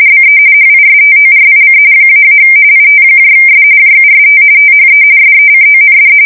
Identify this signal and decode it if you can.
RTTY, about 45.45 baud. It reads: BE HISTORIC -- ITS JUST THE SECOND
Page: 170 Shift 2125/2295